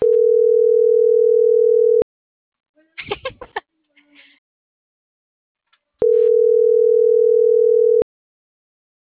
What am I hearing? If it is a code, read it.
Ringback tone (North America), 440 + 480 Hz.